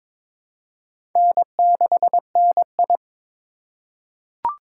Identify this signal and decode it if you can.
Morse code, heard as a keyed tone, about 23 wpm, 700 Hz. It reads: N6NI
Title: USA Call Signs - 3x Character Spacing - 22wpm